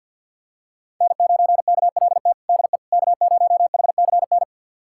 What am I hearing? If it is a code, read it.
Morse code, heard as a keyed tone, about 50 wpm, 700 Hz. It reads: N9KLT DE K0HCN
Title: USA Callsign DE Callsign - 5 Characters - 50wpm